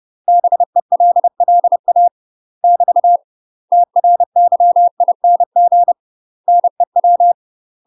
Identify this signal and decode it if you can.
Morse code, heard as a keyed tone, about 30 wpm, 700 Hz. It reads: BELLA <BT> TRYING NEW
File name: UK MASTER QSOs TRAINING / 30WPM / UK MASTER QSOs 0031